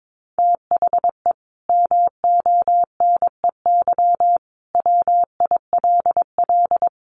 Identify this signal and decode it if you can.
Morse code, heard as a keyed tone, about 22 wpm, 700 Hz. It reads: THE MONEY WILL
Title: Sentences from Top 1500 Words 22wpm